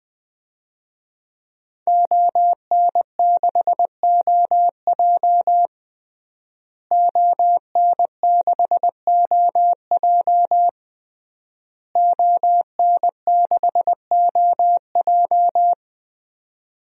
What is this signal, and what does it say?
Morse code, heard as a keyed tone, about 20 wpm, 700 Hz. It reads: ON6OJ ON6OJ ON6OJ
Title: DXCC-100 Call Signs - 5 Characters - Repeated 3x - 20wpm